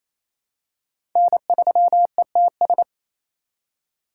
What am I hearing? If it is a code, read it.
Morse code, heard as a keyed tone, about 28 wpm, 700 Hz. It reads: N3ETS
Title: USA Call Signs - 5 Characters - 2x Character Spacing - 28wpm